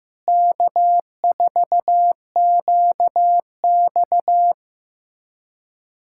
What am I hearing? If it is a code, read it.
Morse code, heard as a keyed tone, about 16 wpm, 700 Hz. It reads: K4QX
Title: USA Callsign DE Callsign - 4 Characters - 15wpm